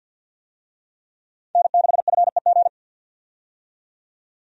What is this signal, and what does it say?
Morse code, heard as a keyed tone, about 52 wpm, 700 Hz. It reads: N6FEG
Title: USA Call Signs - 5 Characters - 50wpm